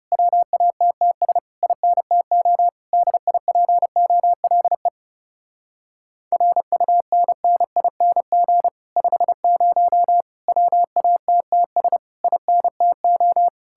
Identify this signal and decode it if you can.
Morse code, heard as a keyed tone, about 34 wpm, 700 Hz. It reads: WATTS INTO DIPOLE RUNNING 50 WATTS INTO
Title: Speed-Racing Common QSO Phrases Encore - 30wpm